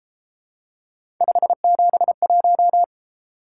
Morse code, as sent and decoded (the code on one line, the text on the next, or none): ..... --... .----
571